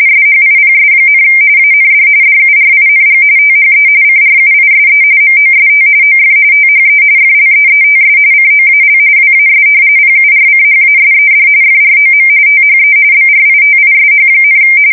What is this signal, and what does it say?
RTTY, about 45.45 baud. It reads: SHIPS, AND DURING THE CRUISE MOST OF THE FLEETS COAL WAS PURCHASED FROM THE BRITISH,
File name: fsk170 / usnavy3